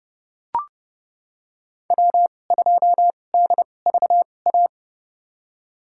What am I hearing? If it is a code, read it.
Morse code, heard as a keyed tone, about 31 wpm, spaced as for 22 wpm, 700 Hz. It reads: W2DVA
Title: USA Call Signs - 5 Characters - 2x Character Spacing - 30wpm